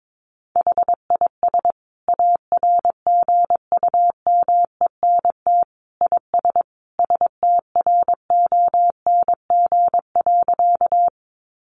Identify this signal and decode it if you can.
Morse code, heard as a keyed tone, about 22 wpm, 700 Hz. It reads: HIS ARGUMENT IS STRONG.
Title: Sentences from Top 1400 Words 22wpm